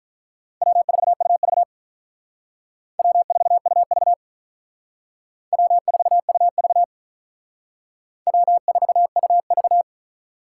Morse code, heard as a keyed tone, about 44 wpm, 700 Hz. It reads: W4UV W4UV W4UV W4UV